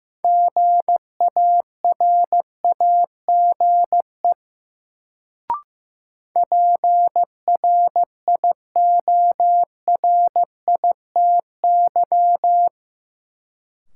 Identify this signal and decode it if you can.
Morse code, heard as a keyed tone, about 15 wpm, 700 Hz. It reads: GARAGE PRIORITY
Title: Sentences from Top 1600 Words - New Word Drill - 15wpm